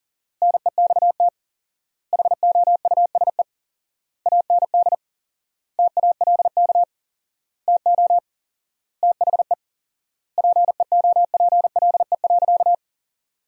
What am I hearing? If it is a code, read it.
Morse code, heard as a keyed tone, about 40 wpm, 700 Hz. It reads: NEXT HOUSE AND TALK TO THE PEOPLE.
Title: 4x Word-Spacing - Sentences from Top 400 Words - 40wpm